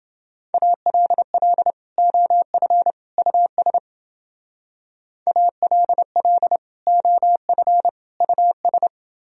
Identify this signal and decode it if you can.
Morse code, heard as a keyed tone, about 29 wpm, 700 Hz. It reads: ALL OF US ALL OF US